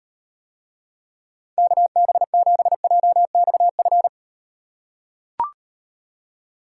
Morse code, heard as a keyed tone, about 38 wpm, 700 Hz. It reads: KB7JXF